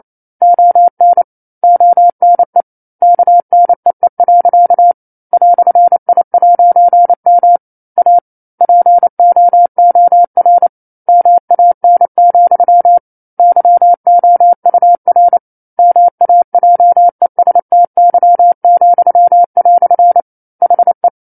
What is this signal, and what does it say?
Morse code, heard as a keyed tone, about 28 wpm, 700 Hz. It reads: ON ONE KNEE. "I'M A POOR MAN, YOUR MAJESTY," HE
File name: 09-Chapter-09 28 WPM